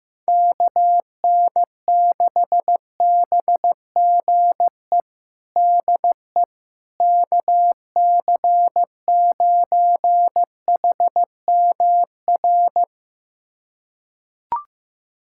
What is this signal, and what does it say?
Morse code, heard as a keyed tone, about 15 wpm, 700 Hz. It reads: KN6BGE DE KC9HMR
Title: USA Callsign DE Callsign 15wpm